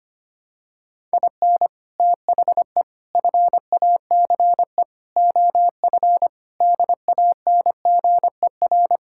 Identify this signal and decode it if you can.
Morse code, heard as a keyed tone, about 25 wpm, 700 Hz. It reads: IN THE FACE OF DANGER
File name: practice / 25wpm / Sentences with 4-Word Phrases - 25wpm